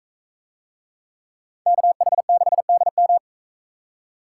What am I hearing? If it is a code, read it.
Morse code, heard as a keyed tone, about 43 wpm, 700 Hz. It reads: KH6DM